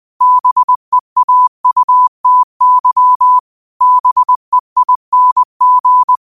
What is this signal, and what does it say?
Morse code, heard as a keyed tone, about 20 wpm, 1000 Hz. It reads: BEAUTY BEING
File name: qotdfiles / 20wpm